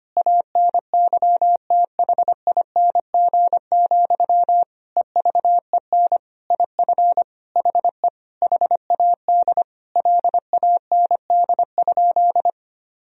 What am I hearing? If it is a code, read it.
Morse code, heard as a keyed tone, about 25 wpm, 700 Hz. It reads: ANYTHING, EVEN IF HE HAD LAND?
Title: Sentences from Top 1000 Words - Review Part 1 - 25wpm